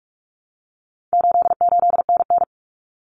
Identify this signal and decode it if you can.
Morse code, heard as a keyed tone, about 44 wpm, 700 Hz. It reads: 88NN
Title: Common Cut Numbers - 4-Digit 45wpm